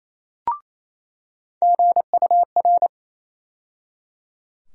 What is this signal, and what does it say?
Morse code, heard as a keyed tone, about 28 wpm, 700 Hz. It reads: GUR
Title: USA Call Sign Postfixes - 3 Letters 28wpm